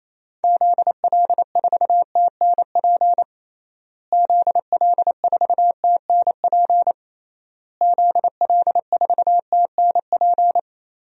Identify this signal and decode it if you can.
Morse code, heard as a keyed tone, about 28 wpm, 700 Hz. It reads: ZL4TNP ZL4TNP ZL4TNP
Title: DXCC-100 Call Signs - 6 Characters - Repeated 3x - 28wpm